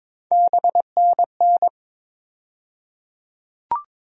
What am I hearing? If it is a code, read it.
Morse code, heard as a keyed tone, about 22 wpm, 700 Hz. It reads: BNN
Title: USA Call Sign Postfixes - 3 Letters 22wpm